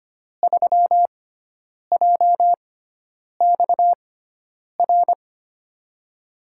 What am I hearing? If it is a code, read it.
Morse code, heard as a keyed tone, about 25 wpm, spaced as for 9 wpm, 700 Hz. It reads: 3JXR
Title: USA Call Signs - 5 Characters - 6x Character Spacing - 25wpm